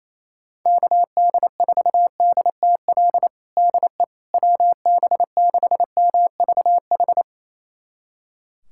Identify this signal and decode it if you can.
Morse code, heard as a keyed tone, about 28 wpm, 700 Hz. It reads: KD4DTL DE WB6MVH
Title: USA Callsign DE Callsign - 6 Characters - 28wpm